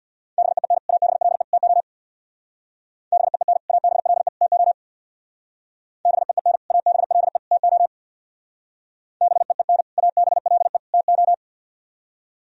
Morse code, heard as a keyed tone, about 61 wpm, 700 Hz. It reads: BEEN ABLE TO BEEN ABLE TO BEEN ABLE TO BEEN ABLE TO
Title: Speed-Racing - Sets of 3 Words 50wpm